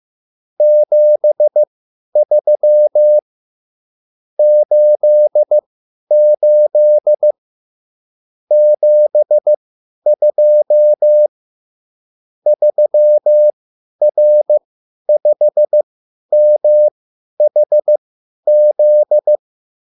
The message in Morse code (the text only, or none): 73 88 72 3R5MHZ